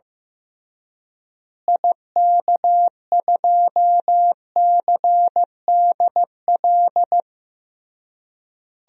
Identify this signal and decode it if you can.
Morse code, heard as a keyed tone, about 15 wpm, 700 Hz. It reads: IK2CDL
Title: DXCC-100 Call Signs - 6 Characters - Repeated 3x - 15wpm